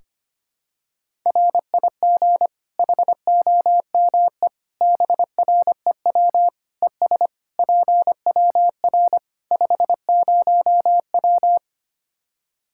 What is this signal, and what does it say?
Morse code, heard as a keyed tone, about 25 wpm, 700 Hz. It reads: RIG HOME BREW ES PWR 50W
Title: K1USN SST QSO Practice 25wpm